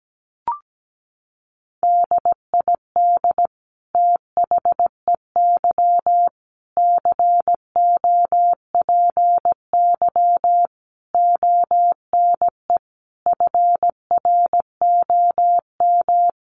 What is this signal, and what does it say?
Morse code, heard as a keyed tone, about 17 wpm, 700 Hz. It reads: DID THEY COPY ONE FROM
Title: Sentences from Top 1400 Words 17wpm